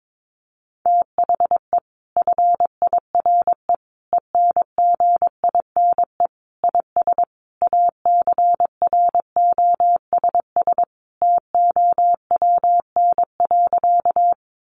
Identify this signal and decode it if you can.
Morse code, heard as a keyed tone, about 22 wpm, 700 Hz. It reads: THE FIRE ENGINE IS ACROSS TOWN.
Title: Sentences from Top 1500 Words 22wpm